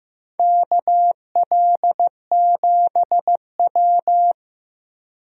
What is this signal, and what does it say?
Morse code, heard as a keyed tone, about 15 wpm, 700 Hz. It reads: KL7W
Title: DXCC-100 Call Signs - 4 Characters - Repeated 3x - 15wpm